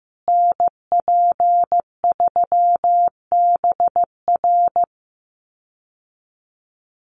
Morse code, heard as a keyed tone, about 15 wpm, 700 Hz. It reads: NP3BR
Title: USA Call Signs Repeated 2x 15wpm